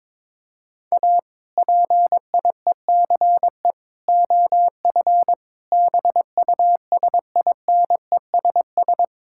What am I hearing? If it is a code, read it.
Morse code, heard as a keyed tone, about 22 wpm, 700 Hz. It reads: A PIECE OF BUSINESS
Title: Sentences from Top 1000 Words - Review Part 5 - 22wpm